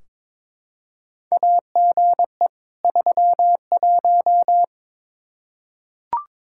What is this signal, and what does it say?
Morse code, heard as a keyed tone, about 22 wpm, 700 Hz. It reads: AGE 31
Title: K1USN SST QSO Practice 22wpm